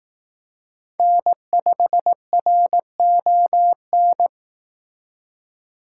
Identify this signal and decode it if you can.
Morse code, heard as a keyed tone, about 18 wpm, 700 Hz. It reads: N5RON
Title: Speed Racing - USA 1x3 Call Signs 15wpm